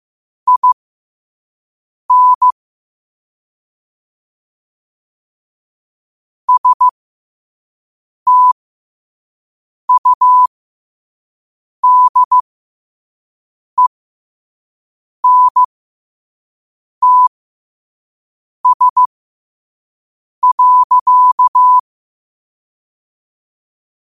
Morse code, heard as a keyed tone, about 15 wpm, spaced as for 5 wpm, 1000 Hz. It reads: IN STUDENTS.
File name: qotdfiles / 5wpm